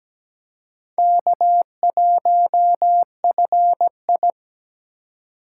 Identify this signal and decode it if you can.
Morse code, heard as a keyed tone, about 20 wpm, 700 Hz. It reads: K1FI
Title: Speed Racing - USA 1x2 Call Signs 17wpm